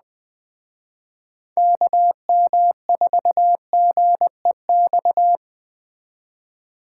Morse code, heard as a keyed tone, about 21 wpm, 700 Hz. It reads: KM4GEX
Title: Speed Racing - USA 2x3 Call Signs 17wpm